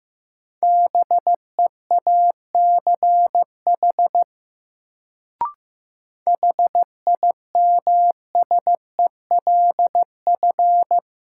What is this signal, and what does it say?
Morse code, heard as a keyed tone, about 15 wpm, 700 Hz. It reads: BEACH HIMSELF
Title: Sentences from Top 1500 Words - New Word Drill Alt - 15wpm